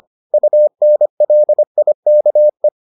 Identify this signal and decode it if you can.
Morse code, heard as a keyed tone, about 25 wpm, 600 Hz. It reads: UNLIKE